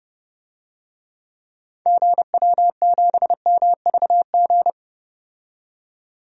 Morse code, heard as a keyed tone, about 32 wpm, 700 Hz. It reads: GW7MVG
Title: DXCC-100 Call Signs - 6 Characters - 30wpm